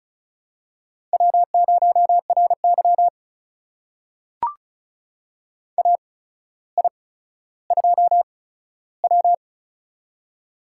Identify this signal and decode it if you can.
Morse code, heard as a keyed tone, about 36 wpm, spaced as for 15 wpm, 700 Hz. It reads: W0RY AI2W
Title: USA Call Signs - 4 Characters - 8x Character Spacing - 35wpm